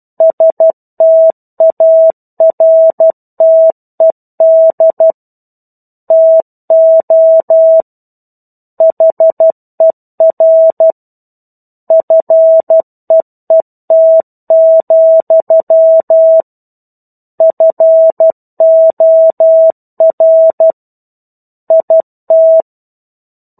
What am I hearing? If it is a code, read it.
Morse code, heard as a keyed tone, about 12 wpm, 650 Hz. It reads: STARTED TO HER FEET, FOR IT
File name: 01-Chapter-01 12 WPM